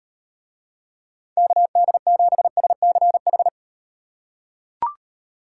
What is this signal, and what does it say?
Morse code, heard as a keyed tone, about 38 wpm, 700 Hz. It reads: KD7SCH